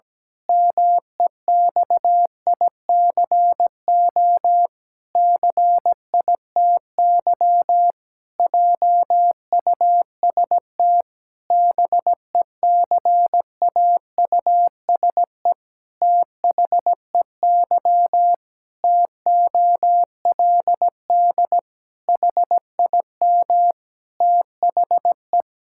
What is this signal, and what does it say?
Morse code, heard as a keyed tone, about 17 wpm, 700 Hz. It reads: MEXICO CITY JUST BECAUSE THEY TOLD HIM THE
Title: Pangram Challenge v2 - 17wpm